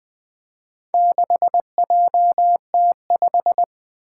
Morse code, heard as a keyed tone, about 20 wpm, 700 Hz. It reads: 6JT5
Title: Four Challenging Letter-Number Combinations 20wpm